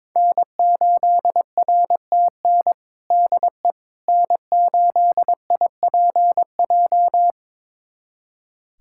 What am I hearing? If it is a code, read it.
Morse code, heard as a keyed tone, about 22 wpm, 700 Hz. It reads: N8RTN DE N8IPJ
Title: USA Callsign DE Callsign - 5 Characters - 22wpm